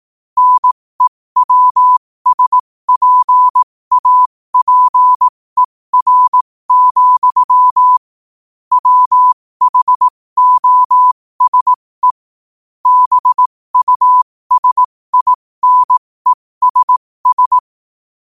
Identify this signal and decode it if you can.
Morse code, heard as a keyed tone, about 18 wpm, spaced as for 15 wpm, 1000 Hz. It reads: NEWSPAPER, WHOSE BUSINESS
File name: qotdfiles / 15wpm